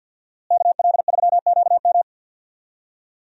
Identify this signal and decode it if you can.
Morse code, heard as a keyed tone, about 47 wpm, 700 Hz. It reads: KL3QM
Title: USA Call Signs - 5 Characters - Repeated 2x - 50wpm